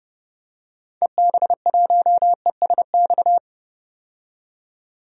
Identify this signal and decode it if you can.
Morse code, heard as a keyed tone, about 31 wpm, 700 Hz. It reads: EB1ESX
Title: DXCC-100 Call Signs - 6 Characters - 30wpm